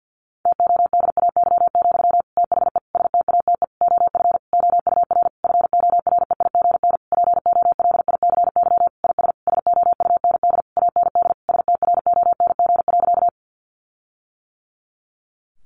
Morse code, heard as a keyed tone, about 50 wpm, 700 Hz. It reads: TODAY, THE STATE OF OUR FOREIGN POLICY IS SOUND AND STRONG.
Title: Sentences from Top 1400 Words 50wpm